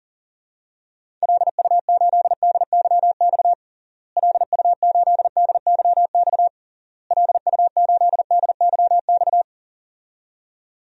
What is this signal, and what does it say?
Morse code, heard as a keyed tone, about 40 wpm, 700 Hz. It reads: LU8DYX LU8DYX LU8DYX
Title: DXCC-100 Call Signs - 6 Characters - Repeated 3x - 40wpm